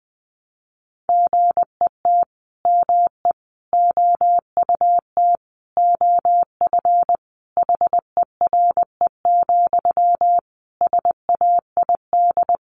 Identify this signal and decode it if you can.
Morse code, heard as a keyed tone, about 20 wpm, 700 Hz. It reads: GET ME OUT OF HERE, SAID
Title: A Sound Of Thunder 20wpm